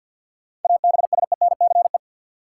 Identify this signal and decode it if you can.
Morse code, heard as a keyed tone, about 48 wpm, 700 Hz. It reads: ABSENCE
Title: Sets of 4 Words - New Word Drill - 50wpm